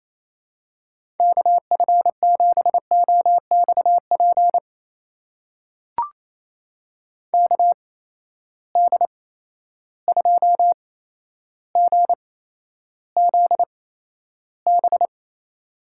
Morse code, heard as a keyed tone, about 28 wpm, spaced as for 12 wpm, 700 Hz. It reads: KF7OXP KD2GZB
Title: USA Call Signs - 8x Character Spacing - 28wpm